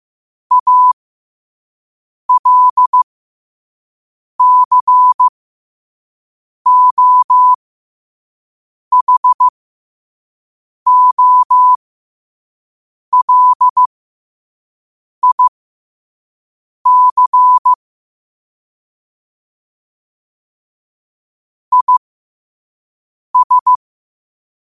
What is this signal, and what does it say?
Morse code, heard as a keyed tone, about 15 wpm, spaced as for 5 wpm, 1000 Hz. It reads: ALCOHOLIC IS
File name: 5wpm